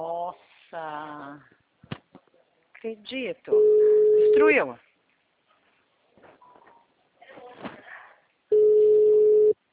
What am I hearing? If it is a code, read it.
Ringback tone (Europe, 425 Hz), with speech.